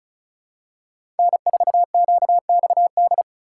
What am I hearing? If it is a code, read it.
Morse code, heard as a keyed tone, about 34 wpm, 700 Hz. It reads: N4QXD
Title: USA Call Signs - 5 Characters - 2x Character Spacing - 35wpm